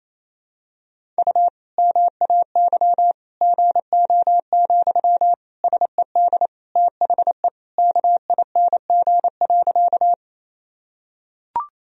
Morse code, heard as a keyed tone, about 28 wpm, 700 Hz. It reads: U MAY GO, SED THE KING.
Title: Sentences from Top 400 Words and Abbreviations 28wpm